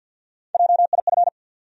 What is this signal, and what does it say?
Morse code, heard as a keyed tone, about 52 wpm, 700 Hz. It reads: JIF
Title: USA Call Sign Postfixes - 3 Letters 50wpm